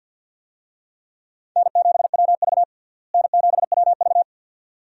Morse code, heard as a keyed tone, about 53 wpm, 700 Hz. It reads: N7WV N7WV
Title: USA Call Signs - 4 Characters - Repeated 2x - 50wpm